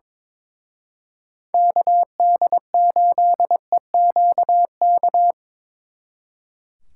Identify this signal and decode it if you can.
Morse code, heard as a keyed tone, about 24 wpm, 700 Hz. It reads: KD8EQK
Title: Speed Racing - USA 2x3 Call Signs 22wpm